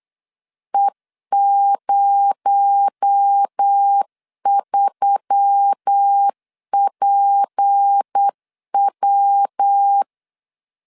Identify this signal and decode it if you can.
Morse code, heard as a keyed tone, about 8 wpm, 800 Hz. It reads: E03PW
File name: LEZIONI / 6 PKX0 / 6 PKX0 8wpm 3341